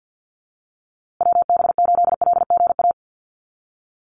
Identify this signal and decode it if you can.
Morse code, heard as a keyed tone, about 48 wpm, 700 Hz. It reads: WB8LGA
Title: USA Call Signs Repeated 2x 50wpm